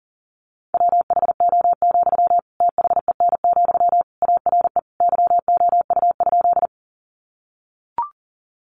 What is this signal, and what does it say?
Morse code, heard as a keyed tone, about 40 wpm, 700 Hz. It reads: WHO, THEN, ARE YOU?
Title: Sentences from Top 100 Words 40wpm